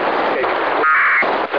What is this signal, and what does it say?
Packet radio (AX.25, as in APRS).
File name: ho-68-12-24-2009